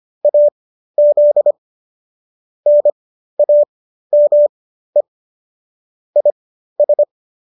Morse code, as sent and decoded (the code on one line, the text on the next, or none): .- --.. / -. .- -- . / .. ...
AZ NAME IS